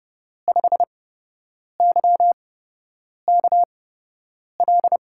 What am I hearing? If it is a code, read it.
Morse code, heard as a keyed tone, about 30 wpm, spaced as for 8 wpm, 700 Hz. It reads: 5YKL
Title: USA Call Signs - 5 Characters - 8x Character Spacing - 30wpm